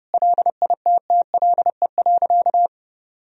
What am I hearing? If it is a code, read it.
Morse code, heard as a keyed tone, about 30 wpm, 700 Hz. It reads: LITTLE.
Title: Sentences from Top 1000 Words - Review Part 3 - 30wpm